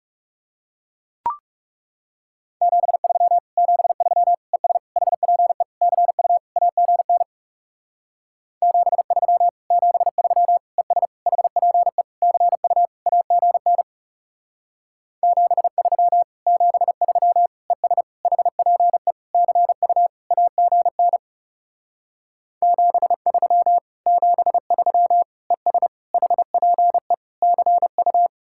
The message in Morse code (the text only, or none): 73 73 ES HPE CU AGN 73 73 ES HPE CU AGN 73 73 ES HPE CU AGN 73 73 ES HPE CU